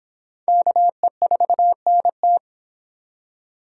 Morse code, heard as a keyed tone, about 24 wpm, 700 Hz. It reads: KE4NT